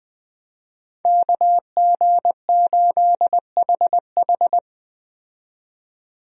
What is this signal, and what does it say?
Morse code, heard as a keyed tone, about 20 wpm, 700 Hz. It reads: KG8HH
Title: USA Call Signs - 5 Characters - 20wpm